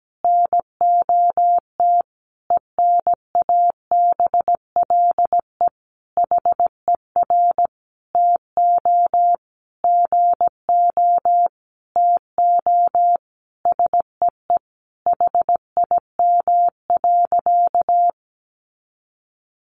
Morse code, heard as a keyed tone, about 17 wpm, 700 Hz. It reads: NOT ENABLE HER TO GO TO SEE HIM.
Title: Sentences from Top 1500 Words 17wpm